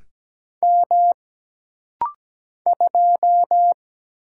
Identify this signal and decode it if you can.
Morse code, heard as a keyed tone, about 17 wpm, 700 Hz. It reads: M 2
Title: Single Letter-Number - Rapid-Fire 17wpm